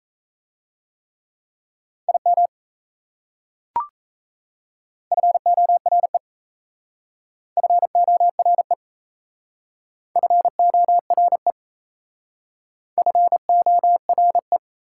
Morse code, heard as a keyed tone, about 34 wpm, 700 Hz. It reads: IM FORE FORE FORE FORE
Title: Speed-Racing Prefixes and Suffixes 28wpm